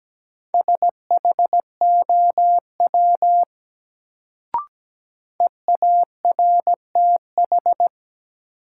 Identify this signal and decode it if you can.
Morse code, heard as a keyed tone, about 17 wpm, 700 Hz. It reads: SHOW EARTH
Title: Sentences from Top 200 Words - New Word Drill - 17wpm